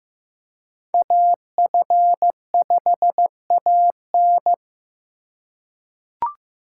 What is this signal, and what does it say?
Morse code, heard as a keyed tone, about 16 wpm, 700 Hz. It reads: AF5AN
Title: USA Call Signs - 4x Character Spacing - 15wpm